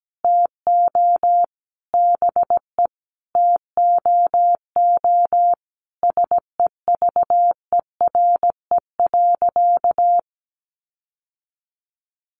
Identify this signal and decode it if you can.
Morse code, heard as a keyed tone, about 17 wpm, 700 Hz. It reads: TO BE TOO SEVERE.
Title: Sentences from Top 1400 Words 17wpm